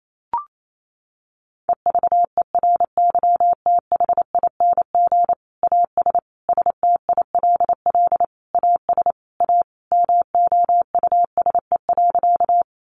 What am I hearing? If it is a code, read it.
Morse code, heard as a keyed tone, about 28 wpm, 700 Hz. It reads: EVERYTHING AS STILL AS A MOUSE.
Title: Sentences from Top 700 Words 28wpm